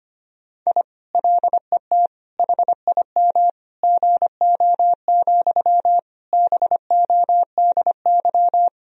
Morse code, heard as a keyed tone, about 25 wpm, 700 Hz. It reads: I LET HIM GO, BODY
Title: Sentences from Top 1000 Words - Review Part 2 - 25wpm